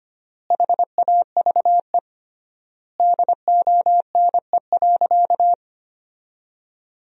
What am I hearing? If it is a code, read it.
Morse code, heard as a keyed tone, about 25 wpm, 700 Hz. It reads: HAVE DONE.
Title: 3x Word-Spacing - Sentences from Top 500 Words - 25wpm